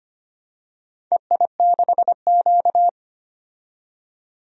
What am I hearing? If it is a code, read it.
Morse code, heard as a keyed tone, about 25 wpm, 700 Hz. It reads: EI6Q
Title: DXCC-100 Call Signs - 4 Characters - 25wpm